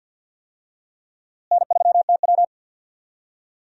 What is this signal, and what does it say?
Morse code, heard as a keyed tone, about 52 wpm, 700 Hz. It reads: N3TW
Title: USA Call Signs - 4 Characters - Repeated 3x - 50wpm